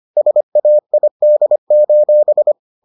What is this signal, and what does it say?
Morse code, heard as a keyed tone, about 25 wpm, 600 Hz. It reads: SAID: